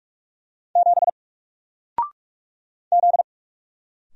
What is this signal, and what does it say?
Morse code, heard as a keyed tone, about 45 wpm, 700 Hz. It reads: 7 Z
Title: Single Letter-Number 45wpm